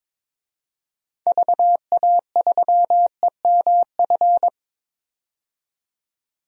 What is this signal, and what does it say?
Morse code, heard as a keyed tone, about 21 wpm, 700 Hz. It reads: VA3EMF
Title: DXCC-100 Call Signs - 6 Characters - 22wpm